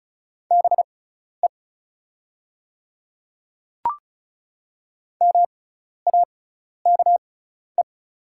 Morse code, heard as a keyed tone, about 34 wpm, spaced as for 12 wpm, 700 Hz. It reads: BE MAKE
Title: 6x Character Spacing - Top 300 Words - 35wpm